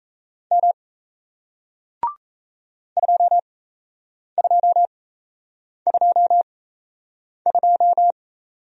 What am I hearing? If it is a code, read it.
Morse code, heard as a keyed tone, about 38 wpm, spaced as for 8 wpm, 700 Hz. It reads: M 2222